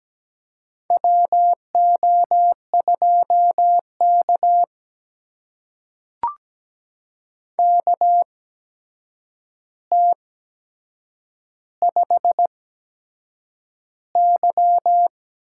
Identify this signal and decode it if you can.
Morse code, heard as a keyed tone, about 17 wpm, spaced as for 7 wpm, 700 Hz. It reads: WO2K KT5Y
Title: USA Call Signs - 8x Character Spacing - 17wpm